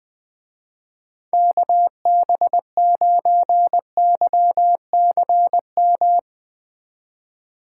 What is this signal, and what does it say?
Morse code, heard as a keyed tone, about 19 wpm, 700 Hz. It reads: KB9YCM
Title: Speed Racing - USA Call Signs 20wpm